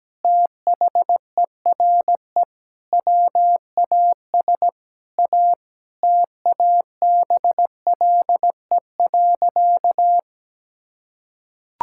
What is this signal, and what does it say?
Morse code, heard as a keyed tone, about 17 wpm, 700 Hz. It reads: THERE WAS A TABLE.
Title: Sentences from Top 1000 Words - Review Part 6 - 17wpm